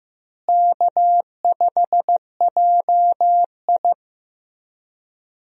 Morse code, heard as a keyed tone, about 15 wpm, 700 Hz. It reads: K5JI
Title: USA Call Signs - 4 Characters - Repeated 3x - 15wpm